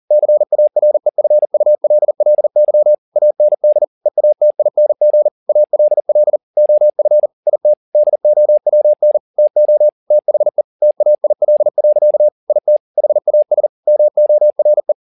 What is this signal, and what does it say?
Morse code, heard as a keyed tone, about 40 wpm, 600 Hz. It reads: CAREFULLY AND EATING ALL OF IT DOWN TO THE TAIL. IT HAS MORE
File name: the old man and the sea 2 40wpm